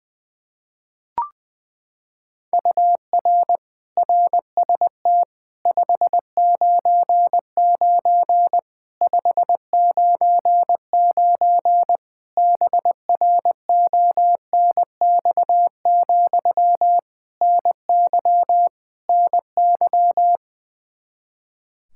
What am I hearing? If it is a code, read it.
Morse code, heard as a keyed tone, about 20 wpm, 700 Hz. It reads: UR RST 599 599 BRONX, NY NY
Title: Practice QSO - Exchange 1 - Beginning - 20wpm